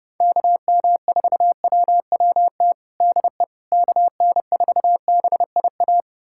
Morse code, heard as a keyed tone, about 30 wpm, 700 Hz. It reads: KM4WWT DE KN4BIA
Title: USA Callsign DE Callsign 30wpm